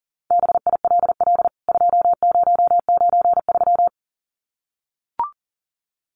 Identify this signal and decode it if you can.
Morse code, heard as a keyed tone, about 40 wpm, 700 Hz. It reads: BILL 2093
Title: CWT Exchange 40wpm